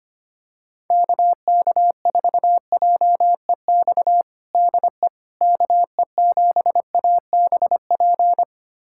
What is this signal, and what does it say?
Morse code, heard as a keyed tone, about 25 wpm, 700 Hz. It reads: KK4JEX DE KE7ABP
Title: USA Callsign DE Callsign - 6 Characters - 25wpm